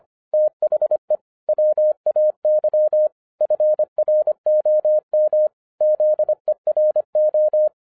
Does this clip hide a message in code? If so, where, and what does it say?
Yes: Morse code from 0.3 to 7.7 s, heard as a keyed tone, about 26 wpm, 610 Hz. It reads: THE WAY FROM ZERO